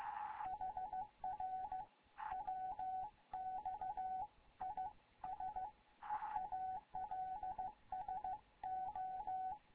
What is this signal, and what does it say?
Morse code, heard as a keyed tone, about 15 wpm, 750 Hz, with radio noise. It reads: HR WX IS ALSO
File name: Morze-HA2MN / hangp / 2 11 GYAKORLAT